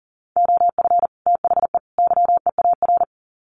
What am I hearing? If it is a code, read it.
Morse code, heard as a keyed tone, about 40 wpm, 700 Hz. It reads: OF THE YEAR
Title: Sets of 3 Words 40wpm